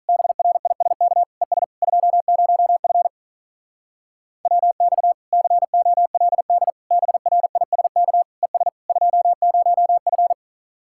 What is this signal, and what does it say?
Morse code, heard as a keyed tone, about 45 wpm, 700 Hz. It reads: BRISK ES 20F WX COLD BRISK ES 20F